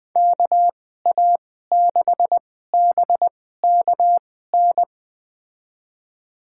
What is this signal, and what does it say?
Morse code, heard as a keyed tone, about 20 wpm, spaced as for 14 wpm, 700 Hz. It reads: KA6BKN
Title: USA Call Signs - 6 Characters - 2x Character Spacing - 20wpm